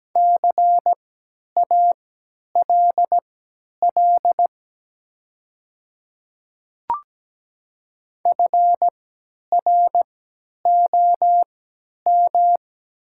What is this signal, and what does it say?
Morse code, heard as a keyed tone, about 17 wpm, spaced as for 10 wpm, 700 Hz. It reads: CALL FROM
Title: 3x Character Spacing - Top 200 Words - 17wpm